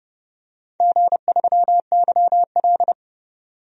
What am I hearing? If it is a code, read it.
Morse code, heard as a keyed tone, about 29 wpm, 700 Hz. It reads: G3YL
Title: DXCC-100 Call Signs - 4 Characters - 30wpm